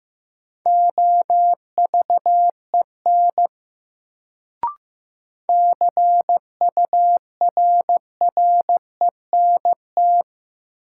Morse code, heard as a keyed tone, about 15 wpm, 700 Hz. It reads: OVEN CURRENT
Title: Sentences from Top 1000 Words - Review Part 4 - New Word Drill - 15wpm